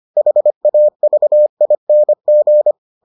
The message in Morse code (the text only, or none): HAVING